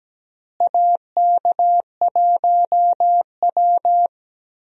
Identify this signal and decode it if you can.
Morse code, heard as a keyed tone, about 17 wpm, 700 Hz. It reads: AK1W
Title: USA Call Signs - 4 Characters - Repeated 3x - 17wpm